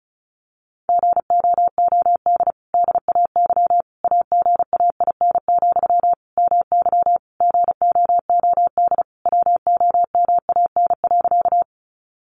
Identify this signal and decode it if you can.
Morse code, heard as a keyed tone, about 34 wpm, 700 Hz. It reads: GOOD DAY AGAIN, MY GOOD WOMAN.
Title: Sentences from Top 1100 Words 35wpm